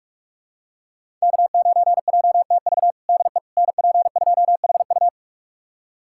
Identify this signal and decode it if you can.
Morse code, heard as a keyed tone, about 45 wpm, 700 Hz. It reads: K9JTV DE NP2HU
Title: USA Callsign DE Callsign - 5 Characters - 45wpm